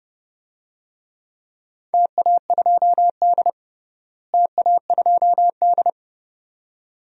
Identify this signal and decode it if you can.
Morse code, heard as a keyed tone, about 32 wpm, 700 Hz. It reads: TA2D TA2D
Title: DXCC-100 Call Signs - 4 Characters - Repeated 2x - 30wpm